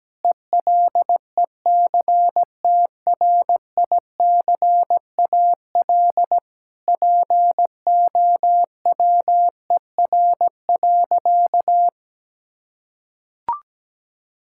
Morse code, heard as a keyed tone, about 17 wpm, 700 Hz. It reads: ELECTRICAL POWER.
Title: Sentences from Top 1000 Words - Review Part 6 - 17wpm